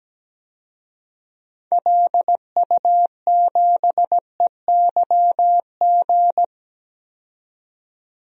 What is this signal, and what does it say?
Morse code, heard as a keyed tone, about 17 wpm, 700 Hz. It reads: LU7EYG
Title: DXCC-100 Call Signs - 6 Characters - 17wpm